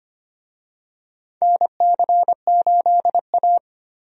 Morse code, heard as a keyed tone, about 25 wpm, 700 Hz. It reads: NC8A
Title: USA Call Signs - 4 Characters - Repeated 3x - 25wpm